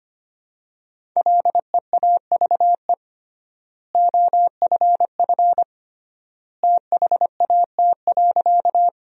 Morse code, heard as a keyed tone, about 25 wpm, 700 Hz. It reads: LEAVE OFF THAT.
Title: 3x Word-Spacing - Sentences from Top 400 Words - 25wpm